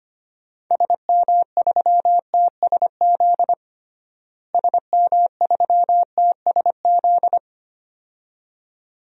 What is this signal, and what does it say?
Morse code, heard as a keyed tone, about 25 wpm, 700 Hz. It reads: SM3TSZ SM3TSZ
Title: DXCC-100 Call Signs - 6 Characters - Repeated 2x - 25wpm